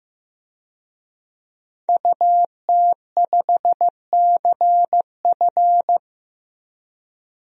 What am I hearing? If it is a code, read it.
Morse code, heard as a keyed tone, about 15 wpm, 700 Hz. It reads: UT5CF
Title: DXCC-100 Call Signs - 5 Characters - 15wpm